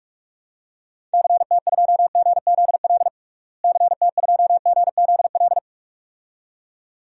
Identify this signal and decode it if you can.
Morse code, heard as a keyed tone, about 42 wpm, 700 Hz. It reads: CT2GZL CT2GZL
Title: DXCC-100 Call Signs - 6 Characters - Repeated 2x - 45wpm